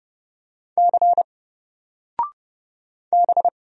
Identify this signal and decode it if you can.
Morse code, heard as a keyed tone, about 29 wpm, 700 Hz. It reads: C B
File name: Single-Letter-Number-30wpm